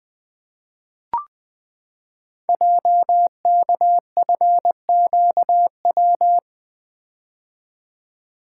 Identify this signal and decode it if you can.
Morse code, heard as a keyed tone, about 20 wpm, 700 Hz. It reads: JKFQW
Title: Code Groups - 5 Letters - 20wpm